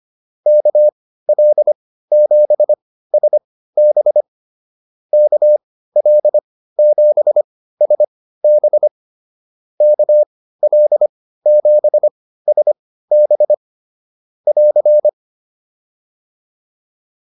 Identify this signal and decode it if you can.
Morse code, heard as a keyed tone, about 25 wpm, spaced as for 15 wpm, 600 Hz. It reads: KL7SB KL7SB KL7SB <AR>
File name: QSO107 15